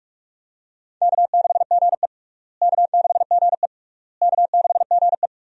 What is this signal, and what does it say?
Morse code, heard as a keyed tone, about 41 wpm, 700 Hz. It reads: K6GE K6GE K6GE
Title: USA Call Signs - 4 Characters - Repeated 3x - 45wpm